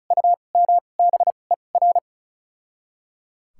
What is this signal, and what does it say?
Morse code, heard as a keyed tone, about 36 wpm, spaced as for 25 wpm, 700 Hz. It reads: UMBER
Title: 2x Character Spacing - Top 100 Words - 35wpm